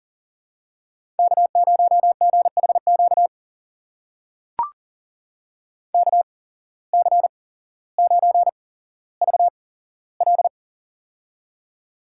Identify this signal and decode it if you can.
Morse code, heard as a keyed tone, about 39 wpm, spaced as for 17 wpm, 700 Hz. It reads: K0GHQ KC9VL
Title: USA Call Signs - 5 Characters - 8x Character Spacing - 40wpm